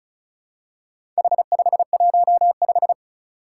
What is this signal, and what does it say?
Morse code, heard as a keyed tone, about 36 wpm, 700 Hz. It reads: H515
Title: Four Challenging Letter-Number Combinations 35wpm